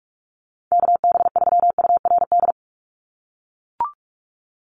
Morse code, heard as a keyed tone, about 44 wpm, 700 Hz. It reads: KB3URD
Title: USA Call Signs 45wpm